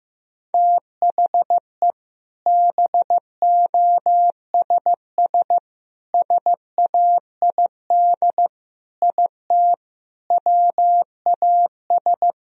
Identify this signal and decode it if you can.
Morse code, heard as a keyed tone, about 16 wpm, 700 Hz. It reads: THE BOSS SAID IT WAS
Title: Sentences from Top 1000 Words - Review Part 4 - 15wpm